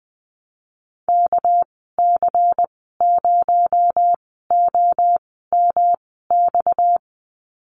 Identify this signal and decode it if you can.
Morse code, heard as a keyed tone, about 20 wpm, spaced as for 14 wpm, 700 Hz. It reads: KC0OMX
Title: 231 Lesson 37g - Letter x - Call Signs 20wpm